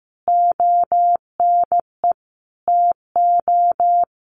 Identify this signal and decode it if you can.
Morse code, heard as a keyed tone, about 15 wpm, 700 Hz. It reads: ONE TO
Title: Sentences from Top 1500 Words 15wpm